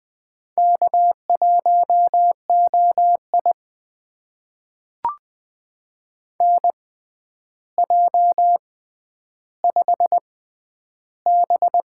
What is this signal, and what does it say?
Morse code, heard as a keyed tone, about 20 wpm, spaced as for 10 wpm, 700 Hz. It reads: K1OI NJ5B
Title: USA Call Signs - 4 Characters - 6x Character Spacing - 20wpm